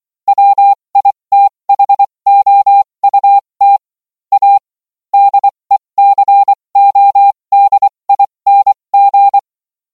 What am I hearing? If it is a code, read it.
Morse code, heard as a keyed tone, about 24 wpm, spaced as for 20 wpm, 800 Hz. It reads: WITHOUT A DECODING